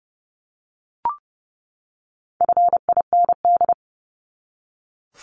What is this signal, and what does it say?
Morse code, heard as a keyed tone, about 30 wpm, 700 Hz. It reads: FIND
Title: ING Suffix - Top 100 Words with Base Words 30wpm